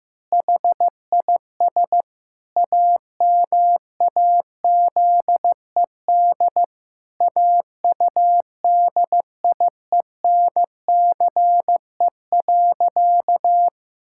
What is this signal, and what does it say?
Morse code, heard as a keyed tone, about 15 wpm, 700 Hz. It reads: HIS AMAZED AUDIENCE.
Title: Pangram Challenge v2 - 15wpm